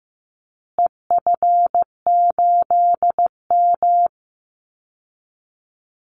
Morse code, heard as a keyed tone, about 15 wpm, 700 Hz. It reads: EF8M
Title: Call Signs Repeated 3x - 15wpm